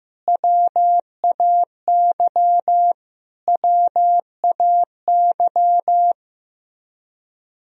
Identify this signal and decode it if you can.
Morse code, heard as a keyed tone, about 15 wpm, 700 Hz. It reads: WAY WAY
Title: Top 100 Words Repeated 15wpm